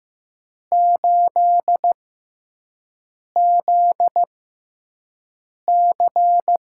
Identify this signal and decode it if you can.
Morse code, heard as a keyed tone, about 15 wpm, spaced as for 5 wpm, 700 Hz. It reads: 8ZC
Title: USA Call Signs - 5 Characters - 6x Character Spacing - 15wpm